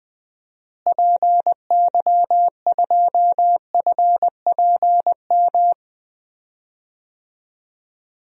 Morse code, heard as a keyed tone, about 21 wpm, 700 Hz. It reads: PY2FPM
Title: DXCC-100 Call Signs - 6 Characters - Repeated 2x - 20wpm